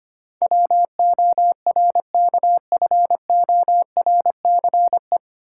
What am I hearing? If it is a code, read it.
Morse code, heard as a keyed tone, about 25 wpm, 700 Hz. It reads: WORKFORCE
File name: Nine Letter Words - 25wpm